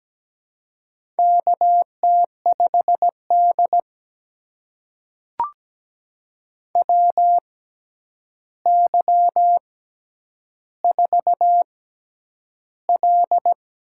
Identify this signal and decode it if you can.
Morse code, heard as a keyed tone, about 17 wpm, spaced as for 9 wpm, 700 Hz. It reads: KT5D WY4L
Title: USA Call Signs - 6x Character Spacing - 17wpm